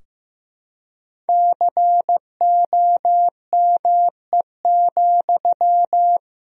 Morse code, heard as a keyed tone, about 15 wpm, 700 Hz. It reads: COME,
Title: Sentences from Top 1000 Words - Review Part 2 - 15wpm